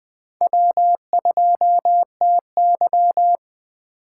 Morse code, heard as a keyed tone, about 20 wpm, 700 Hz. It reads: W2TY
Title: Speed Racing - USA 1x2 Call Signs 17wpm